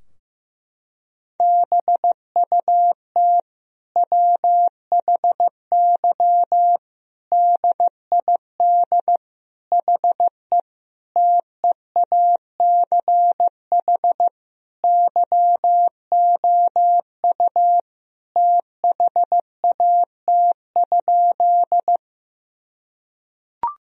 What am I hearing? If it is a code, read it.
Morse code, heard as a keyed tone, about 14 wpm, 700 Hz. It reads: BUT WHY DID HE TEACH YOU THAT?
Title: Sentences from Top 1000 Words - Review Part 2 - 15wpm